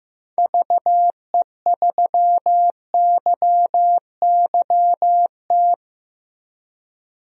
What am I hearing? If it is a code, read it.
Morse code, heard as a keyed tone, about 15 wpm, 700 Hz. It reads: VE3YYT
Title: DXCC-100 Call Signs - 6 Characters - 15wpm